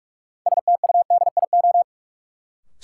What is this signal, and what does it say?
Morse code, heard as a keyed tone, about 41 wpm, 700 Hz. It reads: STUDIO
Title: Sentences from Top 1200 Words - New Word Drill - 45wpm